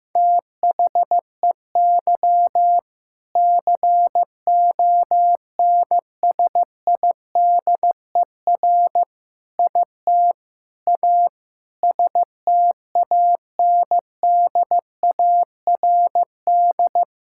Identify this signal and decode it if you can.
Morse code, heard as a keyed tone, about 15 wpm, 700 Hz. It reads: THEY CONSIDER IT A STANDARD
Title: Sentences from Top 1000 Words - Review Part 6 - 15wpm